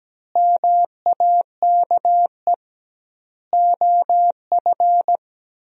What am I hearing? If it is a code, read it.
Morse code, heard as a keyed tone, about 17 wpm, 700 Hz. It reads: MAKE OF
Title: 2x Word-Spacing - Sentences from Top 100 Words - 17wpm